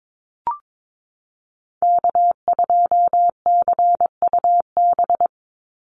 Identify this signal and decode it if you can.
Morse code, heard as a keyed tone, about 22 wpm, 700 Hz. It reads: K2CUB
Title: Call Signs - 22wpm